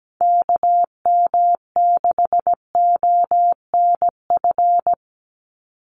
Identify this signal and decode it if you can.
Morse code, heard as a keyed tone, about 17 wpm, 700 Hz. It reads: KM6ONF
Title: USA Call Signs 17wpm